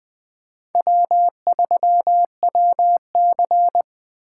Morse code, heard as a keyed tone, about 20 wpm, 700 Hz. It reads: W3WC
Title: Speed Racing - USA 1x2 Call Signs 17wpm